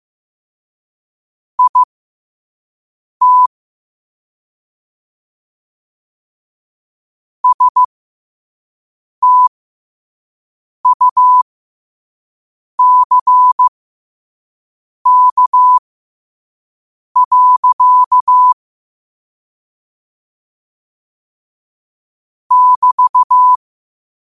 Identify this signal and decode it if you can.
Morse code, heard as a keyed tone, about 15 wpm, spaced as for 5 wpm, 1000 Hz. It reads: IT STUCK. <BT>
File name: qotdfiles / 5wpm